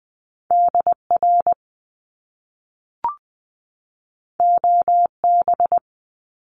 Morse code, heard as a keyed tone, about 20 wpm, 700 Hz. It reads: DR OB
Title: Top Two Letter Combinations in Words 20wpm